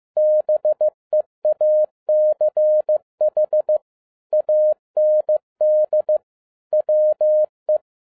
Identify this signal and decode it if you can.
Morse code, heard as a keyed tone, about 15 wpm, 610 Hz. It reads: BEACH AND WE